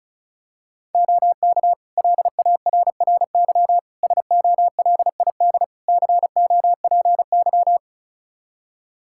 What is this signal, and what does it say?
Morse code, heard as a keyed tone, about 34 wpm, 700 Hz. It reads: OK LARRY SOLID COPY
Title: Practice QSO - Exchange 3 - More Info - 35wpm